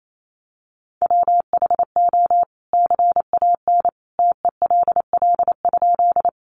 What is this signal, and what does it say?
Morse code, heard as a keyed tone, about 28 wpm, 700 Hz. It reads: WHO CAN TELL?
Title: A Sound Of Thunder 28wpm